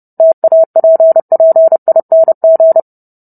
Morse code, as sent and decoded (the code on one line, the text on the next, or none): - .- .--. .--. .. -. --.
TAPPING